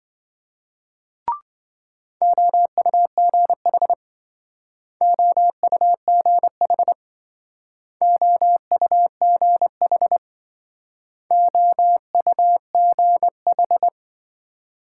Morse code, heard as a keyed tone, about 27 wpm, 700 Hz. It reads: OUGH OUGH OUGH OUGH